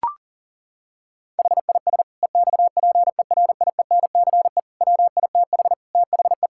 Morse code, heard as a keyed tone, about 37 wpm, 700 Hz. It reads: HIS EXPERIENCE WITH THE
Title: Sentences from Top 1000 Words - Review Part 6 - 40wpm